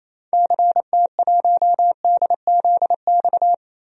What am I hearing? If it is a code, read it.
Morse code, heard as a keyed tone, about 28 wpm, 700 Hz. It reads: CT1DZX
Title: DXCC-100 Call Signs - 6 Characters - 28wpm